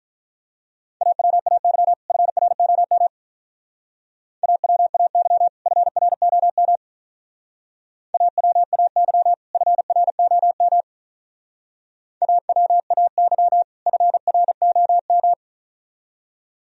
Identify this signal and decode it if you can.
Morse code, heard as a keyed tone, about 44 wpm, 700 Hz. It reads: AWAY FROM AWAY FROM AWAY FROM AWAY FROM